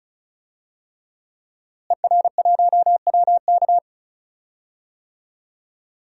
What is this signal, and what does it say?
Morse code, heard as a keyed tone, about 34 wpm, 700 Hz. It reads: ER1WK
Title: DXCC-100 Call Signs - 5 Characters - Repeated 3x - 35wpm